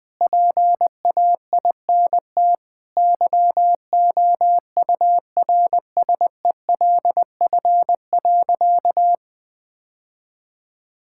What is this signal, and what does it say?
Morse code, heard as a keyed tone, about 21 wpm, 700 Hz. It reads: PAINT YOURSELF.
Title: Sentences from Top 1000 Words - Review Part 6 - 20wpm